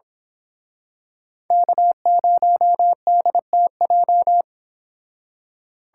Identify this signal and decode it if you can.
Morse code, heard as a keyed tone, about 26 wpm, 700 Hz. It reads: K0DTJ
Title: Speed Racing - USA 1x3 Call Signs 22wpm